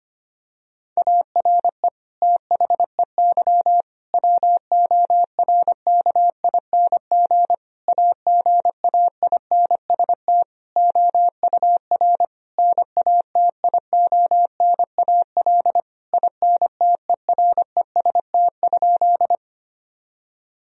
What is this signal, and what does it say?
Morse code, heard as a keyed tone, about 25 wpm, 700 Hz. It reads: ARE THEY WORKING AGAINST OUR NATIONAL INTEREST?
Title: Sentences from Top 1000 Words - Review Part 4 - 25wpm